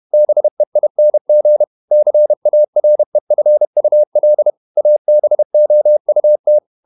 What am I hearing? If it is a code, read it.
Morse code, heard as a keyed tone, about 31 wpm, 600 Hz. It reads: BEING CAREFUL ABOUT